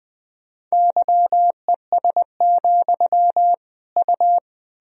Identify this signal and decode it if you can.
Morse code, heard as a keyed tone, about 20 wpm, 700 Hz. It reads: YES, U
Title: Sentences from Top 500 Words and Abbreviations 20wpm